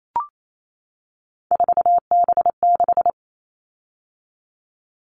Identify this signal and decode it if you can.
Morse code, heard as a keyed tone, about 28 wpm, 700 Hz. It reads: 4B6
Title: All Cut Numbers - 3-Digit 28wpm